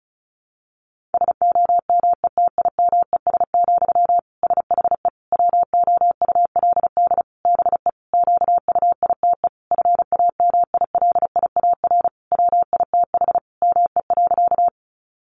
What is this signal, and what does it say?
Morse code, heard as a keyed tone, about 34 wpm, 700 Hz. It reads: SOMETIMES, SHE WOULD BE QUITE FAMILIAR WITH ME.
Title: Sentences from Top 1000 Words 35wpm